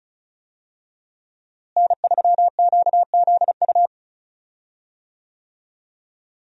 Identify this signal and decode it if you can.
Morse code, heard as a keyed tone, about 36 wpm, 700 Hz. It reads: N3QZU
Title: USA Call Signs - 5 Characters - Repeated 3x - 35wpm